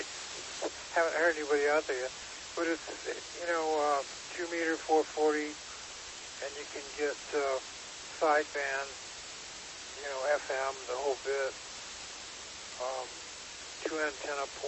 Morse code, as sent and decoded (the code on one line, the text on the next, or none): none